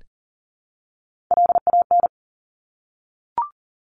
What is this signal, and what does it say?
Morse code, heard as a keyed tone, about 38 wpm, 700 Hz. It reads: LAN
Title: Top Three Letter Combinations in Words 40wpm